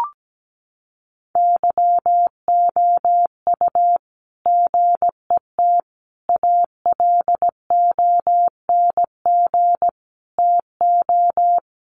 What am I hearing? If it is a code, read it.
Morse code, heard as a keyed tone, about 17 wpm, 700 Hz. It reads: YOU GET ALONG TO
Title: Sentences from Top 700 Words 17wpm